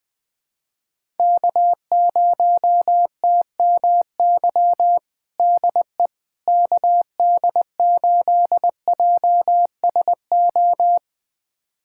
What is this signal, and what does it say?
Morse code, heard as a keyed tone, about 20 wpm, 700 Hz. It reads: K0TMY DE KD8JSO
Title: USA Callsign DE Callsign 20wpm